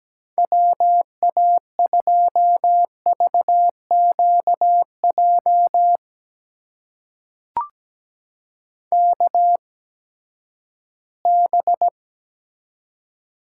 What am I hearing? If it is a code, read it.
Morse code, heard as a keyed tone, about 17 wpm, spaced as for 12 wpm, 700 Hz. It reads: WA2VQJ KB
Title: USA Call Signs - 6 Characters - 8x Character Spacing - 17wpm